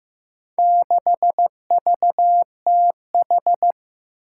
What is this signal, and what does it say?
Morse code, heard as a keyed tone, about 15 wpm, 700 Hz. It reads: 6VTH
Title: Four Challenging Letter-Number Combinations 15wpm